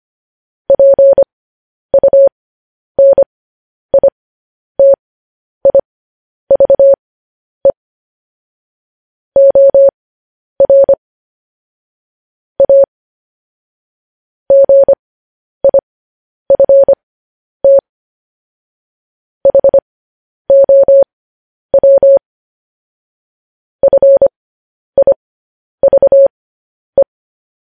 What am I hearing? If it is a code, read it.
Morse code, heard as a keyed tone, about 25 wpm, spaced as for 10 wpm, 550 Hz. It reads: PUNITIVE OR A GIFT HOW FIVE
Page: News Headlines in Morse Code at 10 WPM